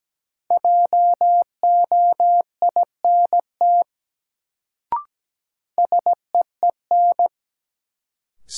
Morse code, heard as a keyed tone, about 17 wpm, 700 Hz. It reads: JOINT SEEN
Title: Sets of 5 Words Encore - New Word Drill - 17wpm